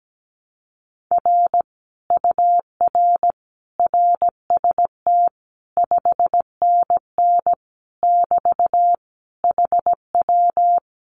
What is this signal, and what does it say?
Morse code, heard as a keyed tone, about 17 wpm, 700 Hz. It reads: R UR RST 5NN <BT> HW
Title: Common QSO Phrases Encore 17wpm